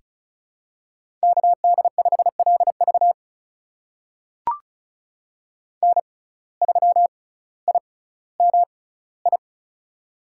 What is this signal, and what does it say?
Morse code, heard as a keyed tone, about 36 wpm, spaced as for 18 wpm, 700 Hz. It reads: KD5LV N3IMI
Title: USA Call Signs - 5 Characters - 6x Character Spacing - 35wpm